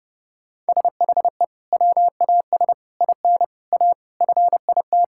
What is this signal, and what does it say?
Morse code, heard as a keyed tone, about 30 wpm, 700 Hz. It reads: SHE WAS IN A FIT
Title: Sentences from Top 1000 Words - Review Part 5 - 30wpm